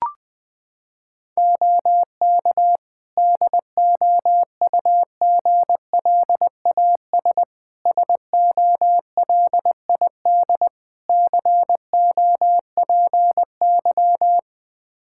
Morse code, heard as a keyed tone, about 20 wpm, 700 Hz. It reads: OK DOUGLAS SOLID COPY
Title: Practice QSO - Exchange 3 - More Info - 20wpm